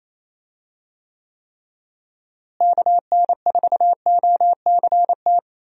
Morse code, heard as a keyed tone, about 28 wpm, 700 Hz. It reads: KN4OCT
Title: USA Call Signs - 6 Characters - Repeated 3x - 28wpm